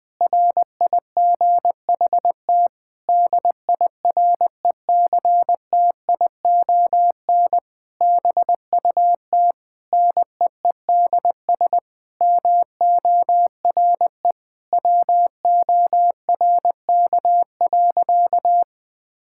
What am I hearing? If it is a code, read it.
Morse code, heard as a keyed tone, about 21 wpm, 700 Hz. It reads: RIGHT DIRECTION BUT NEEDS MORE WORK.
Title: Sentences with 8-Word Phrases - 20wpm